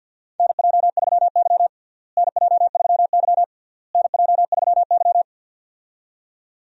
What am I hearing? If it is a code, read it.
Morse code, heard as a keyed tone, about 52 wpm, 700 Hz. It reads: NJ3Y NJ3Y NJ3Y
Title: USA Call Signs - 4 Characters - Repeated 3x - 50wpm